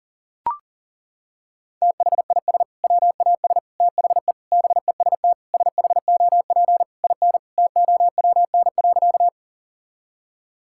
Morse code, heard as a keyed tone, about 40 wpm, 700 Hz. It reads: THIS WAS THE BEST SHOP IN TOWN.
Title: Sentences from Top 1600 Words 40wpm